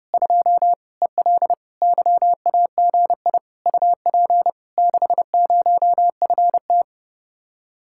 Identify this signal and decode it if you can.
Morse code, heard as a keyed tone, about 32 wpm, 700 Hz. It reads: 2 EL YAGI UP 60FT